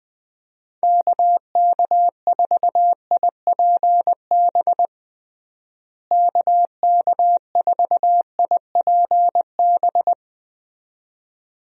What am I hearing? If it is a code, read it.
Morse code, heard as a keyed tone, about 20 wpm, 700 Hz. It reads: KK4IPB KK4IPB
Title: USA Call Signs - 6 Characters - Repeated 2x - 20wpm